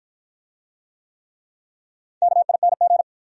Morse code, heard as a keyed tone, about 53 wpm, 700 Hz. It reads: KING